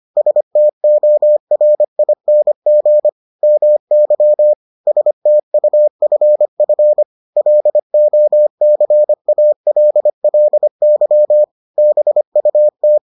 Morse code, heard as a keyed tone, about 25 wpm, 600 Hz. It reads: STORING MY STUFF LOCALLY BUT